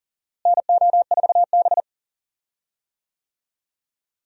Morse code, heard as a keyed tone, about 37 wpm, 700 Hz. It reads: NO4B
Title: USA Call Signs - 4 Characters - Repeated 2x - 40wpm